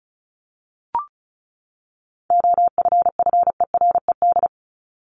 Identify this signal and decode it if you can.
Morse code, heard as a keyed tone, about 34 wpm, 700 Hz. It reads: OFFERED
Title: ED Suffix - Top 100 Words with Base Words 35wpm